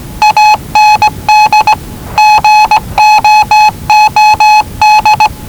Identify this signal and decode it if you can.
Morse code, heard as a keyed tone, about 19 wpm, 850 Hz. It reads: AND GOOD
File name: WellingtonRadioZlwCloseDownOnHfAndMf / 01-Zlw 4277khz 417.5 qrt until 0925z